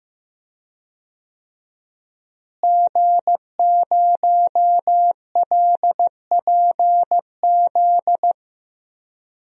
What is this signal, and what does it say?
Morse code, heard as a keyed tone, about 15 wpm, 700 Hz. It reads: G0LPZ
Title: DXCC-100 Call Signs - 5 Characters - Repeated 3x - 15wpm